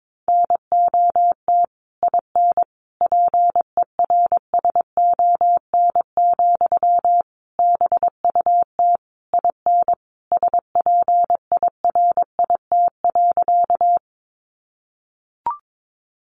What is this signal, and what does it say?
Morse code, heard as a keyed tone, about 22 wpm, 700 Hz. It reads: NOT IN PERSON, BUT IN SPIRIT.
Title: Sentences from Top 1400 Words 22wpm